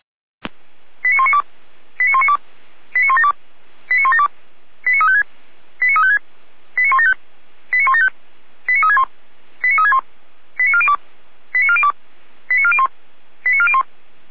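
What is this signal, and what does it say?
Five-tone selective call (ZVEI), heard as quick runs of five tones, tones of about 70 ms. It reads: 89192 89192 89172 89172 89356 89356 89175 89175 89361 89361 89402 89402 89401 89401
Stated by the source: ZVEI five-tone selective call received on an SDR, from the Feuerwehr Orlamünde audio server.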